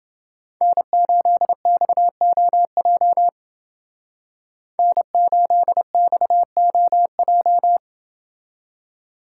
Morse code, heard as a keyed tone, about 28 wpm, 700 Hz. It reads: N8XOJ N8XOJ